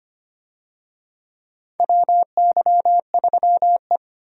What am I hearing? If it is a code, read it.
Morse code, heard as a keyed tone, about 25 wpm, 700 Hz. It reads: WY3E
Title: USA Call Signs - 4 Characters - Repeated 3x - 25wpm